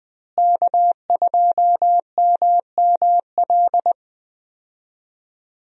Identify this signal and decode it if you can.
Morse code, heard as a keyed tone, about 21 wpm, 700 Hz. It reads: K2MML
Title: USA Call Signs - 5 Characters - Repeated 3x - 20wpm